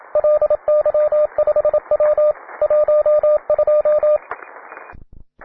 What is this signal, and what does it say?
Morse code, heard as a keyed tone, about 27 wpm, 600 Hz, with radio noise. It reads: LY5W 12